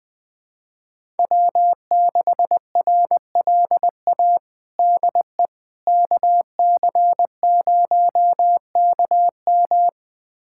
Morse code, heard as a keyed tone, about 20 wpm, 700 Hz. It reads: W6RLA DE KC0KM
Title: USA Callsign DE Callsign - 5 Characters - 20wpm